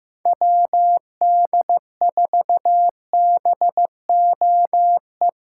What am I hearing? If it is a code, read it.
Morse code, heard as a keyed tone, about 15 wpm, 700 Hz. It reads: WD4BOE
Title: USA Callsign DE Callsign 15wpm